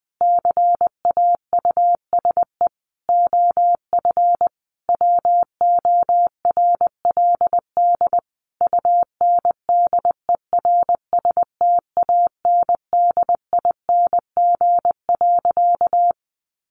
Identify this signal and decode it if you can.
Morse code, heard as a keyed tone, about 20 wpm, 700 Hz. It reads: CAUSE OF WORLD UNDERSTANDING.
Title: Sentences from Top 1400 Words 20wpm